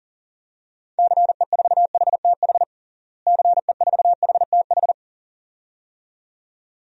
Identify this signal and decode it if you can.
Morse code, heard as a keyed tone, about 40 wpm, 700 Hz. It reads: CE4HTH CE4HTH
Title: DXCC-100 Call Signs - 6 Characters - Repeated 2x - 40wpm